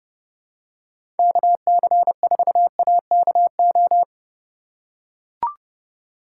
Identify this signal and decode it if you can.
Morse code, heard as a keyed tone, about 30 wpm, 700 Hz. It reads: KC4AKO
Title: USA Call Signs - 6 Characters - 4x Character Spacing - 30wpm